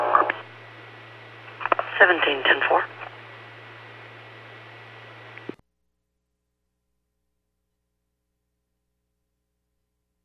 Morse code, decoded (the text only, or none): none